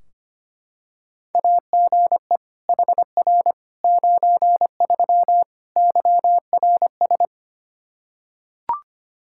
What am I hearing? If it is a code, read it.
Morse code, heard as a keyed tone, about 25 wpm, 700 Hz. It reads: AGE HR 93 YRS
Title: Practice QSO - Exchange 3 - More Info - 25wpm